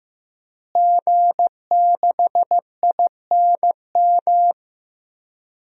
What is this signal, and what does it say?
Morse code, heard as a keyed tone, about 15 wpm, 700 Hz. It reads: G6INM
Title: DXCC-100 Call Signs - 5 Characters - Repeated 2x - 15wpm